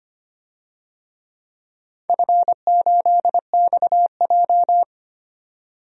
Morse code, heard as a keyed tone, about 25 wpm, 700 Hz. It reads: F8XJ
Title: DXCC-100 Call Signs - 4 Characters - 25wpm